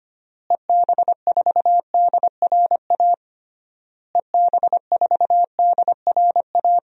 Morse code, heard as a keyed tone, about 25 wpm, 700 Hz. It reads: EB4DRA EB4DRA
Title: DXCC-100 Call Signs - 6 Characters - Repeated 3x - 25wpm